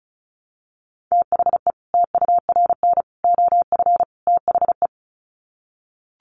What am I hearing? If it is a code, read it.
Morse code, heard as a keyed tone, about 34 wpm, 700 Hz. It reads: THE TURN OF THE
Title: Sets of 4 Words 35wpm